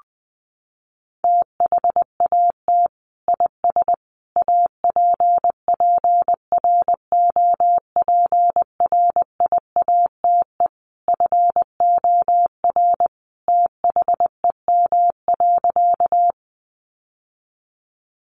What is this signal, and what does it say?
Morse code, heard as a keyed tone, about 20 wpm, 700 Hz. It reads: THAT IS APPROPRIATE FOR THEM.
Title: Sentences from Top 900 Words 20wpm